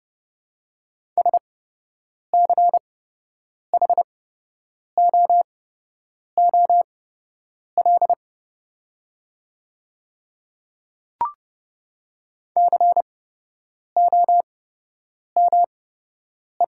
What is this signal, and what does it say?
Morse code, heard as a keyed tone, about 30 wpm, spaced as for 8 wpm, 700 Hz. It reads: SCHOOL COME
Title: 8x Character Spacing - Top 500 Words - 30wpm